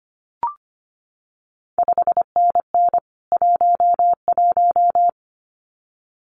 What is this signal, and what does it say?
Morse code, heard as a keyed tone, about 25 wpm, 700 Hz. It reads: 5NN 11
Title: CQ WW DX Contest Exchange Encore - 25wpm